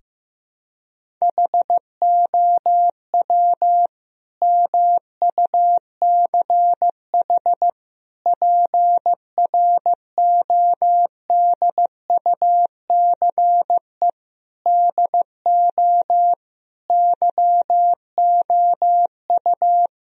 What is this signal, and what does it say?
Morse code, heard as a keyed tone, about 15 wpm, 700 Hz. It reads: HOW MUCH PRODUCE DO YOU
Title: Sentences from Top 1000 Words - Review Part 3 - 15wpm